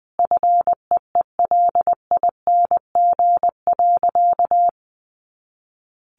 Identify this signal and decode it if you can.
Morse code, heard as a keyed tone, about 20 wpm, 700 Hz. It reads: FEELING.
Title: Additional Easy Phrases 20wpm